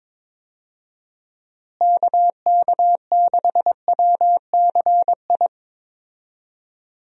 Morse code, heard as a keyed tone, about 22 wpm, 700 Hz. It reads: KK6WCI
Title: USA Call Signs - 6 Characters - Repeated 2x - 22wpm